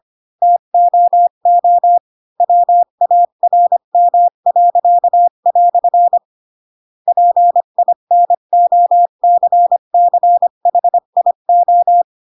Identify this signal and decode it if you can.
Morse code, heard as a keyed tone, about 25 wpm, 700 Hz. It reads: TOO WARM." PINOCCHIO
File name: Pinocchio / Pinocchio ch8 20 25 ik1hge